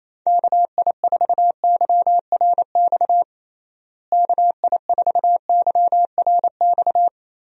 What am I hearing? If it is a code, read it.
Morse code, heard as a keyed tone, about 28 wpm, 700 Hz. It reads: KI4YRX KI4YRX
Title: USA Call Signs - 6 Characters - Repeated 2x - 28wpm